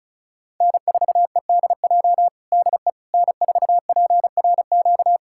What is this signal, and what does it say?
Morse code, heard as a keyed tone, about 34 wpm, 700 Hz. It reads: N4EDJ DE N4PRQ
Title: USA Callsign DE Callsign - 5 Characters - 35wpm